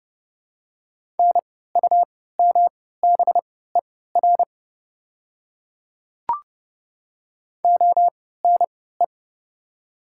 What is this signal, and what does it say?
Morse code, heard as a keyed tone, about 30 wpm, spaced as for 17 wpm, 700 Hz. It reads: NUMBER ONE
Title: 3x Character Spacing - Top 100 Words - 30wpm